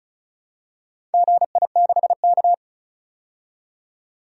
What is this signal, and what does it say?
Morse code, heard as a keyed tone, about 34 wpm, 700 Hz. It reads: GI6K
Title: DXCC-100 Call Signs - 4 Characters - 35wpm